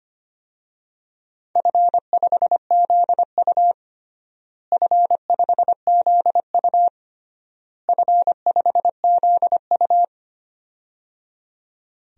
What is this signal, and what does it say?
Morse code, heard as a keyed tone, about 25 wpm, 700 Hz. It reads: F5ZU F5ZU F5ZU
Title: DXCC-100 Call Signs - 4 Characters - Repeated 3x - 25wpm